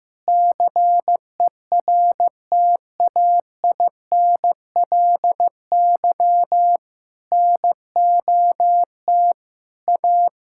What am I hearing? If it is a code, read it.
Morse code, heard as a keyed tone, about 15 wpm, 700 Hz. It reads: CERTAINLY NOT A
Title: Sentences from Top 1000 Words - Review Part 5 - 15wpm